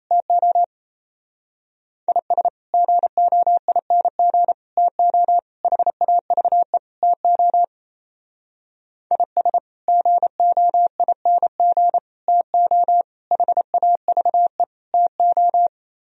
Morse code, heard as a keyed tone, about 35 wpm, 700 Hz. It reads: TO IS GOING TO HAVE TO IS GOING TO HAVE TO
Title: Speed-Racing - Sets of 5 Words 28wpm